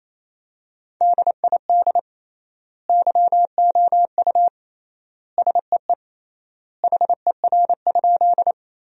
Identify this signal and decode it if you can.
Morse code, heard as a keyed tone, about 28 wpm, 700 Hz. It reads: DID YOU SEE HER?
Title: 3x Word-Spacing - Sentences from Top 100 Words - 28wpm